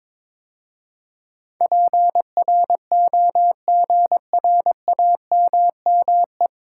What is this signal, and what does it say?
Morse code, heard as a keyed tone, about 22 wpm, 700 Hz. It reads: PROGRAMME
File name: practice / 22wpm / 9 Letter Words - First 3 Letters Given - 22wpm